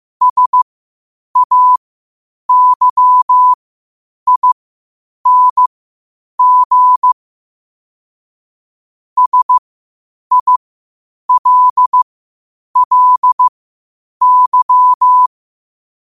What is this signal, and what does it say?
Morse code, heard as a keyed tone, about 15 wpm, spaced as for 8 wpm, 1000 Hz. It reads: SAYING SILLY
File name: qotdfiles / 8wpm